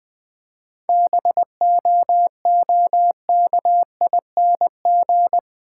Morse code, heard as a keyed tone, about 20 wpm, 690 Hz. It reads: BOOKING
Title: 7 Letter Words - First 3 Letters Given - 20wpm